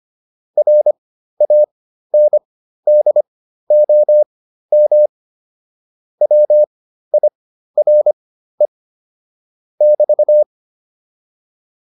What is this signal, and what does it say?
Morse code, heard as a keyed tone, about 25 wpm, spaced as for 13 wpm, 600 Hz. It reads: RANDOM WIRE <BT>